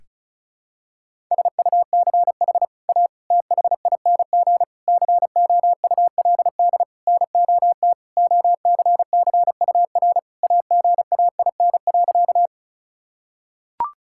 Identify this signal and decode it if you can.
Morse code, heard as a keyed tone, about 36 wpm, 700 Hz. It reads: SUCH A THING COULD NOT OCCUR AGAIN.
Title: Sentences from Top 1000 Words - Review Part 6 - 35wpm